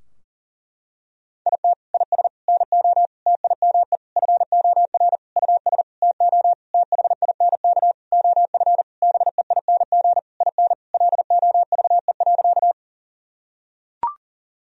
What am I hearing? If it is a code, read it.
Morse code, heard as a keyed tone, about 40 wpm, 700 Hz. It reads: IT IS NO TIME FOR US TO THINK OF BEING IN LOVE.
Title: Sentences from Top 1000 Words - Review Part 3 - 40wpm